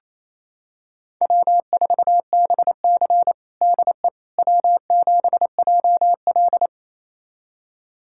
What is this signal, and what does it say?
Morse code, heard as a keyed tone, about 28 wpm, 700 Hz. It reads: W4BC DE W7JL
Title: USA Callsign DE Callsign - 4 Characters - 28wpm